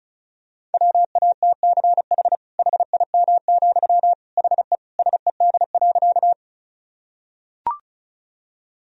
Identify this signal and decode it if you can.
Morse code, heard as a keyed tone, about 36 wpm, 700 Hz. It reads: WATCH HIM, HE SED.
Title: Sentences from Top 300 Words and Abbreviations 35wpm